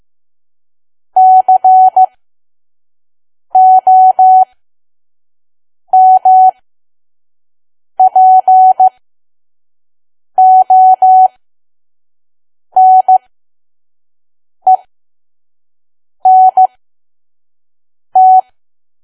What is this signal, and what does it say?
Morse code, heard as a keyed tone, about 15 wpm, spaced as for 5 wpm, 750 Hz. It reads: COMPONENT
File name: Archive / 5 WPM files / PM